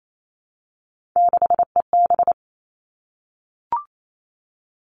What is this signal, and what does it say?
Morse code, heard as a keyed tone, about 28 wpm, 700 Hz. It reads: 6EB
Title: Three Challenging Letter-Number Combinations 28wpm